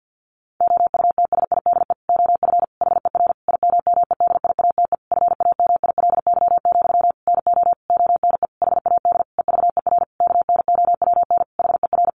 Morse code, heard as a keyed tone, about 50 wpm, 700 Hz. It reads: OUTSIDE OF HER IMMEDIATE FAMILY, NO ONE HAD EVER KNOWN HER
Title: Sentences from Top 1200 Words 50wpm